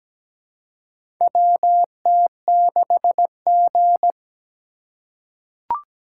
Morse code, heard as a keyed tone, about 17 wpm, 700 Hz. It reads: WT6G
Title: USA Call Signs - 4 Characters - 2x Character Spacing - 17wpm